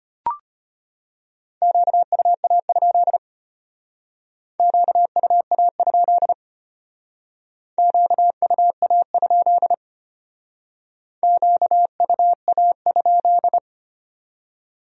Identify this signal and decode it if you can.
Morse code, heard as a keyed tone, about 32 wpm, 700 Hz. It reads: QUA? QUA? QUA? QUA?